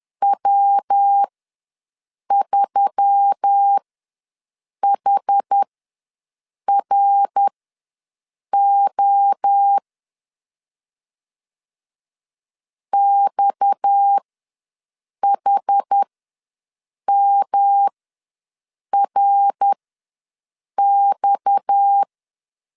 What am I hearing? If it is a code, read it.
Morse code, heard as a keyed tone, about 10 wpm, spaced as for 6 wpm, 800 Hz. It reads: W3HRO XHMRX
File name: LEZIONI / 6 PKX0 / 6 PKX ws 10wpm 3706